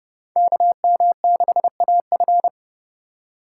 Morse code, heard as a keyed tone, about 30 wpm, 700 Hz. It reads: KM6AF
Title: USA Call Signs - 2x Character Spacing - 30wpm